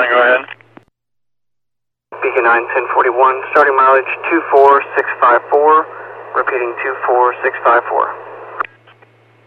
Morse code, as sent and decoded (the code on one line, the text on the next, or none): none